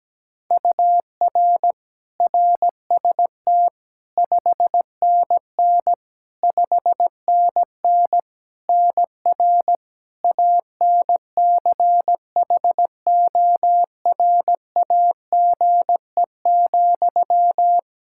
Morse code, heard as a keyed tone, about 17 wpm, 700 Hz. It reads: UR RST 5NN 5NN NR ANCHORAGE,
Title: Practice QSO - Exchange 1 - Beginning - 17wpm